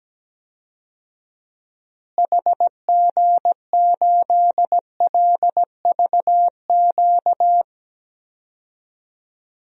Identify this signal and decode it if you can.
Morse code, heard as a keyed tone, about 17 wpm, 700 Hz. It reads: HG8LVQ
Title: DXCC-100 Call Signs - 6 Characters - Repeated 3x - 17wpm